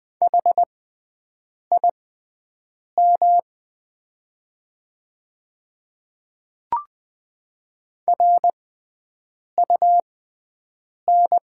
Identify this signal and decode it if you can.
Morse code, heard as a keyed tone, about 20 wpm, spaced as for 7 wpm, 700 Hz. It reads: HIM RUN
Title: 6x Character Spacing - Top 300 Words - 20wpm